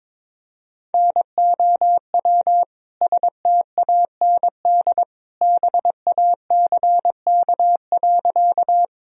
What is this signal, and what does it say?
Morse code, heard as a keyed tone, about 22 wpm, 700 Hz. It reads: NOW STAND BACK.
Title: Sentences from Top 1000 Words - Review Part 1 - 22wpm